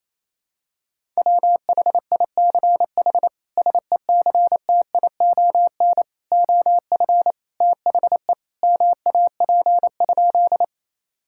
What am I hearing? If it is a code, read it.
Morse code, heard as a keyed tone, about 28 wpm, 700 Hz. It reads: WHICH SECTION OF THE MAP?
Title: Sentences from Top 1000 Words - Review Part 5 - 28wpm